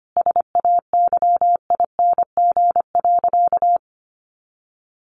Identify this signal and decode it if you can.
Morse code, heard as a keyed tone, about 25 wpm, 700 Hz. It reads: SAYING.
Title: A Sound Of Thunder 25wpm